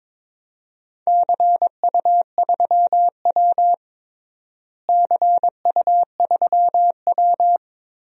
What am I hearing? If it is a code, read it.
Morse code, heard as a keyed tone, about 21 wpm, 700 Hz. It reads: CU3W CU3W
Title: DXCC-100 Call Signs - 4 Characters - Repeated 3x - 22wpm